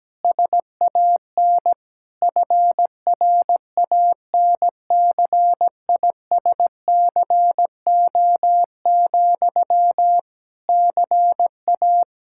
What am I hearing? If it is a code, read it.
Morse code, heard as a keyed tone, about 17 wpm, 700 Hz. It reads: SAN FRANCISCO, CA
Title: Practice QSO - Exchange 1 - Beginning - 17wpm